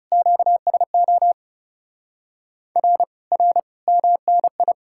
Morse code, heard as a keyed tone, about 40 wpm, 700 Hz. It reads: QSO R R MNI